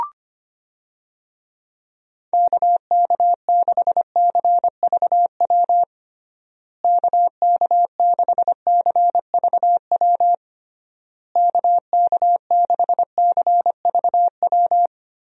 Morse code, heard as a keyed tone, about 25 wpm, 700 Hz. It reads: KK6CVW KK6CVW KK6CVW
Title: USA Call Signs - 6 Characters - Repeated 3x - 25wpm